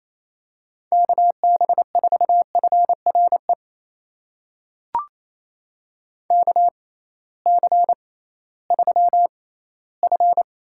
Morse code, heard as a keyed tone, about 28 wpm, spaced as for 16 wpm, 700 Hz. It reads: KB4FRE KC3F
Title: USA Call Signs - 6 Characters - 6x Character Spacing - 28wpm